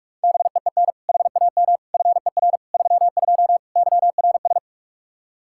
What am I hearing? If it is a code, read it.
Morse code, heard as a keyed tone, about 41 wpm, 700 Hz. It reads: BEEN HAM FER 32 YRS
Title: K1USN SST QSO Practice 45wpm